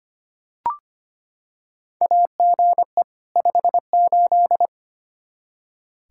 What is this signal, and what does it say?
Morse code, heard as a keyed tone, about 25 wpm, 700 Hz. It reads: AGE 58
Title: Practice QSO - Exchange 3 - More Info - 25wpm